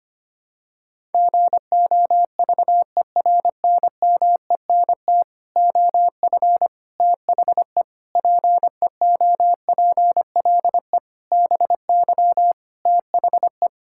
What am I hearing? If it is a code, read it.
Morse code, heard as a keyed tone, about 25 wpm, 700 Hz. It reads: GOVERNMENT OF THE PEOPLE BY THE
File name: practice / 25wpm / Sentences with 9-Word Phrases - 25wpm